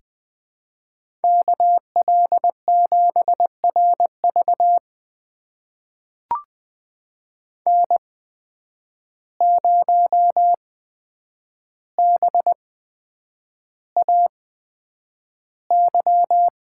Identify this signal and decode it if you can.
Morse code, heard as a keyed tone, about 20 wpm, spaced as for 9 wpm, 700 Hz. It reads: KL7RV N0BAY
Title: USA Call Signs - 5 Characters - 8x Character Spacing - 20wpm